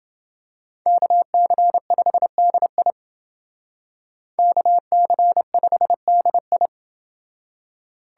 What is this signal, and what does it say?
Morse code, heard as a keyed tone, about 29 wpm, 700 Hz. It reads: KC5DI KC5DI